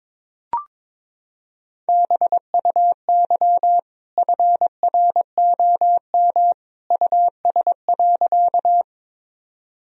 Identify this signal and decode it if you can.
Morse code, heard as a keyed tone, about 22 wpm, 700 Hz. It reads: BUY FROM US.
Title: Sentences from Top 1000 Words - Review Part 4 - 22wpm